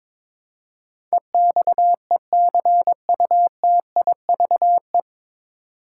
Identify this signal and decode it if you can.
Morse code, heard as a keyed tone, about 22 wpm, 700 Hz. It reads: EXECUTIVE
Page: Morse Code Ninja practice 22wpm